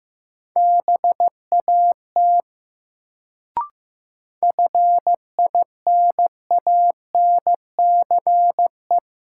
Morse code, heard as a keyed tone, about 15 wpm, 700 Hz. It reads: BAT FINANCE
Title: Sentences from Top 1500 Words - New Word Drill Alt - 15wpm